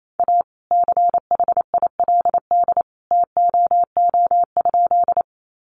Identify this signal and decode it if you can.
Morse code, heard as a keyed tone, about 28 wpm, 700 Hz. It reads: A CHILD TOO?
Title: Sentences from Top 800 Words 28wpm